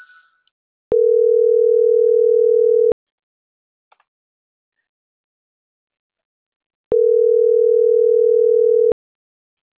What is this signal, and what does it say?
Ringback tone (North America), 440 + 480 Hz.